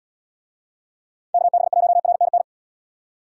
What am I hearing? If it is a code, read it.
Morse code, heard as a keyed tone, about 78 wpm, 700 Hz. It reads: KB1RNN